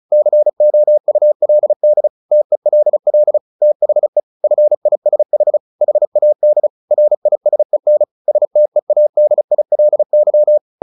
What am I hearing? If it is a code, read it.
Morse code, heard as a keyed tone, about 34 wpm, 600 Hz. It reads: COULD TELL THE FISH HAD RISEN STEADILY
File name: the old man and the sea 2 35wpm